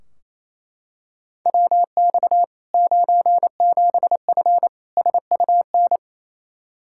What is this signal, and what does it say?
Morse code, heard as a keyed tone, about 27 wpm, 700 Hz. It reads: WX 97F SUN
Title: Practice QSO - Exchange 2 - Info - 28wpm